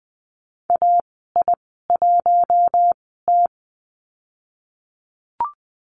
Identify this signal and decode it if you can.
Morse code, heard as a keyed tone, about 20 wpm, spaced as for 15 wpm, 700 Hz. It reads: AI1T
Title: 042 Lesson 8g - Number 1 - Call Signs 20wpm